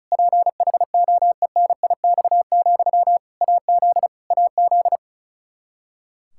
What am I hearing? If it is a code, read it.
Morse code, heard as a keyed tone, about 36 wpm, 700 Hz. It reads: PHOENIX, AZ AZ
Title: Practice QSO - Exchange 1 - Beginning - 35wpm